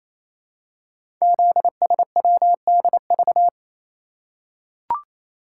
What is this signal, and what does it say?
Morse code, heard as a keyed tone, about 29 wpm, 700 Hz. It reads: ZSWDV
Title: Code Groups - 5 Letters - 28wpm